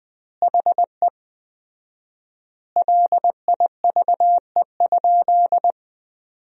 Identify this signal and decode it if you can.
Morse code, heard as a keyed tone, about 20 wpm, 700 Hz. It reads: HE LIVE?
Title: 4x Word-Spacing - Sentences from Top 200 Words - 20wpm